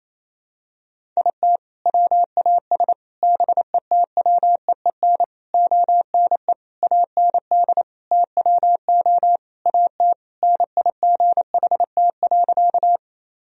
Morse code, heard as a keyed tone, about 28 wpm, 700 Hz. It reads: IT WAS BETWEEN ONE AND TWO AT NIGHT.
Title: Sentences from Top 1000 Words - Review Part 1 - 28wpm